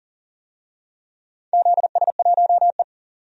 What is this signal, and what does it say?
Morse code, heard as a keyed tone, about 37 wpm, 700 Hz. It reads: ZS1E
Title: DXCC-100 Call Signs - 4 Characters - 40wpm